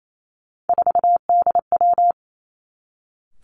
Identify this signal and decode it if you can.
Morse code, heard as a keyed tone, about 28 wpm, 700 Hz. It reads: 4DW
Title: Three Challenging Letter-Number Combinations 28wpm